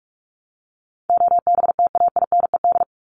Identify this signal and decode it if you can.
Morse code, heard as a keyed tone, about 44 wpm, 700 Hz. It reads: OBTAINED
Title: ED Suffix - Top 100 Words with Base Words 45wpm